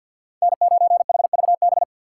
Morse code, heard as a keyed tone, about 46 wpm, 700 Hz. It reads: N9HVB
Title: USA Call Signs - 5 Characters - 6x Character Spacing - 50wpm